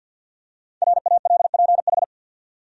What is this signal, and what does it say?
Morse code, heard as a keyed tone, about 46 wpm, 700 Hz. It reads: RALPH
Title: USA Common Names - 50wpm